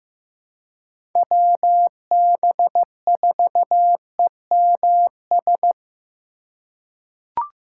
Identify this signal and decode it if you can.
Morse code, heard as a keyed tone, about 15 wpm, 700 Hz. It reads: WB4EMS
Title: USA Call Signs - 2x Character Spacing - 15wpm